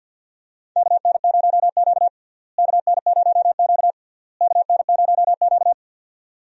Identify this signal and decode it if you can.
Morse code, heard as a keyed tone, about 50 wpm, 700 Hz. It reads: KN0Q KN0Q KN0Q
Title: USA Call Signs - 4 Characters - Repeated 3x - 50wpm